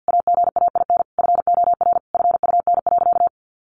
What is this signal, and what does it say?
Morse code, heard as a keyed tone, about 50 wpm, 700 Hz. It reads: AGAIN FOR FUN.
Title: Easy Phrases 50wpm